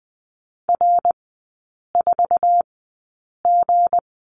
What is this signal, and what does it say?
Morse code, heard as a keyed tone, about 20 wpm, spaced as for 8 wpm, 700 Hz. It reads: R4G
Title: 166 Lesson 27e - Number 7 - Three Characters 20wpm